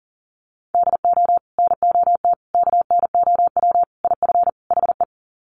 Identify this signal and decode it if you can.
Morse code, heard as a keyed tone, about 40 wpm, 700 Hz. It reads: DO NOT KNOW IF HE
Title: Sets of 5 Words Encore2 40wpm